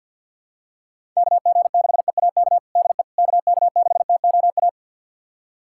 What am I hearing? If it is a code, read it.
Morse code, heard as a keyed tone, about 51 wpm, 700 Hz. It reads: KG6EAK DE KK6TOA
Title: USA Callsign DE Callsign - 6 Characters - 50wpm